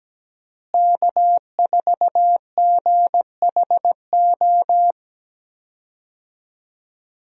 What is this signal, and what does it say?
Morse code, heard as a keyed tone, about 17 wpm, 700 Hz. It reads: K4GHO
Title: USA Call Signs - 5 Characters - Repeated 2x - 17wpm